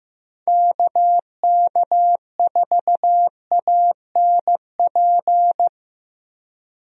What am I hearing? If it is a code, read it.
Morse code, heard as a keyed tone, about 15 wpm, 700 Hz. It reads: KK4ANP
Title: USA Call Signs - 6 Characters - Repeated 2x - 15wpm